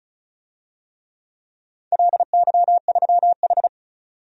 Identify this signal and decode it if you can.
Morse code, heard as a keyed tone, about 36 wpm, 700 Hz. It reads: LY3H
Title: DXCC-100 Call Signs - 4 Characters - 35wpm